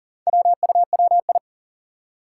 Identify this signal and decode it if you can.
Morse code, heard as a keyed tone, about 40 wpm, 700 Hz. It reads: WUWI
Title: Four Challenging Letter-Number Combinations 40wpm